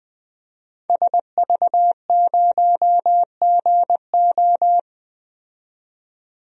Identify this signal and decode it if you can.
Morse code, heard as a keyed tone, about 20 wpm, 700 Hz. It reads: SV0GO
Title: DXCC-100 Call Signs - 5 Characters - 20wpm